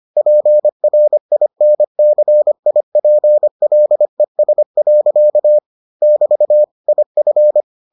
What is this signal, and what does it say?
Morse code, heard as a keyed tone, about 25 wpm, 600 Hz. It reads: PRINCIPLES. <BT> IF